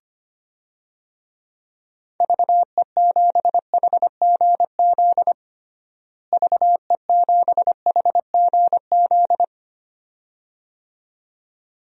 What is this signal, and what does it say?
Morse code, heard as a keyed tone, about 25 wpm, 700 Hz. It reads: VE7HGZ VE7HGZ
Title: DXCC-100 Call Signs - 6 Characters - Repeated 2x - 25wpm